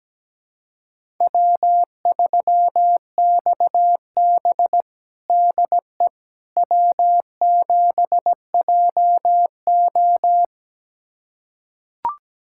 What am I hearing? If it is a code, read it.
Morse code, heard as a keyed tone, about 17 wpm, 700 Hz. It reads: W3XB DE W7JO
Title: USA Callsign DE Callsign - 4 Characters - 17wpm